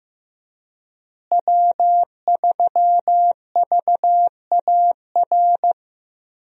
Morse code, heard as a keyed tone, about 15 wpm, 700 Hz. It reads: W3VAR
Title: USA Call Signs - 5 Characters - Repeated 3x - 15wpm